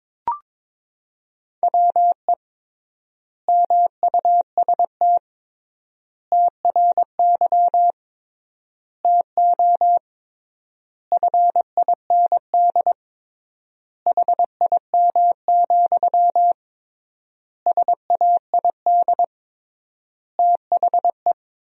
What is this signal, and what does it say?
Morse code, heard as a keyed tone, about 21 wpm, 700 Hz. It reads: WE MUST TRY TO FIND HIM, SAID THE
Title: 3x Word-Spacing - Sentences from Top 400 Words - 22wpm